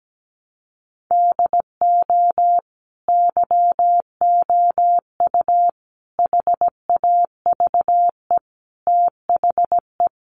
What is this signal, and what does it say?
Morse code, heard as a keyed tone, about 17 wpm, 700 Hz. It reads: DO YOU HAVE THE
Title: Sentences from Top 800 Words 17wpm